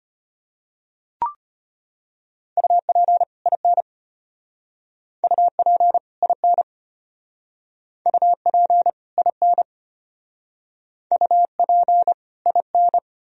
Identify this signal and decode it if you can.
Morse code, heard as a keyed tone, about 32 wpm, 700 Hz. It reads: UP IN UP IN UP IN UP IN